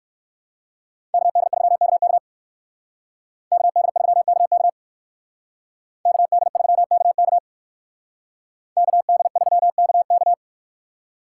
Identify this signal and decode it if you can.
Morse code, heard as a keyed tone, about 64 wpm, 700 Hz. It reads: KD3KK KD3KK KD3KK KD3KK